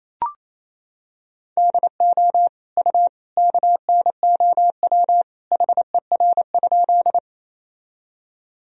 Morse code, heard as a keyed tone, about 27 wpm, 700 Hz. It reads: DO U KNOW HER?
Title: Sentences from Top 100 Words and Abbreviations 28wpm